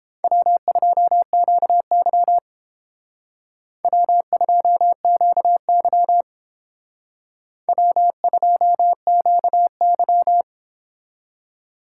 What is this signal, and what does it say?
Morse code, heard as a keyed tone, about 30 wpm, 700 Hz. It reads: W2QY W2QY W2QY